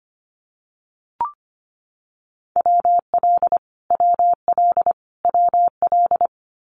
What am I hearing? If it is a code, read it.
Morse code, heard as a keyed tone, about 25 wpm, 700 Hz. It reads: WL WL WL
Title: Abbreviations Repeated 3x - 25wpm